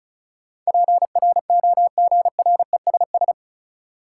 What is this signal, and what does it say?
Morse code, heard as a keyed tone, about 36 wpm, 700 Hz. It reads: PROGRESS
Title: Sentences from Top 1400 Words - New Word Drill Alt - 35wpm